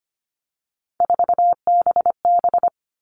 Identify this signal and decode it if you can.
Morse code, heard as a keyed tone, about 25 wpm, 700 Hz. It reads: 4BB
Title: All Cut Numbers - 3-Digit 25wpm